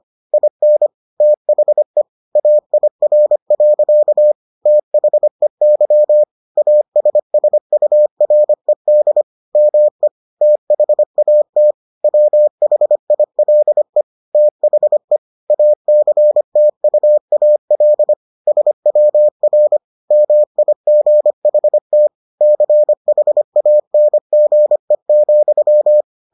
Morse code, heard as a keyed tone, about 25 wpm, 600 Hz. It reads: IN THE AIR. THEY ASSURED ME THAT WHILE THE ACTUAL SWR MIGHT CHANGE,